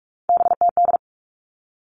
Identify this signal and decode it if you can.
Morse code, heard as a keyed tone, about 44 wpm, 700 Hz. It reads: BTD
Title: All Cut Numbers - 3-Digit 45wpm